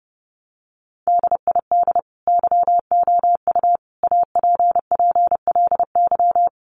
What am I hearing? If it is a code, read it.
Morse code, heard as a keyed tone, about 30 wpm, 700 Hz. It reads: DID YOU APPLY
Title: Sentences from Top 1300 Words 30wpm